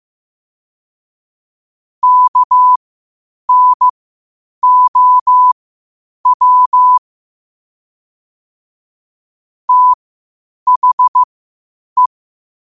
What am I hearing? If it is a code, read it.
Morse code, heard as a keyed tone, about 15 wpm, spaced as for 8 wpm, 1000 Hz. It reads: KNOW THE
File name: qotdfiles / 8wpm